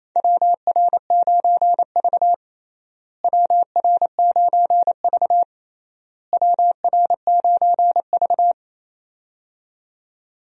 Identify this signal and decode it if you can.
Morse code, heard as a keyed tone, about 28 wpm, 700 Hz. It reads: WR9V WR9V WR9V
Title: USA Call Signs - 4 Characters - Repeated 3x - 28wpm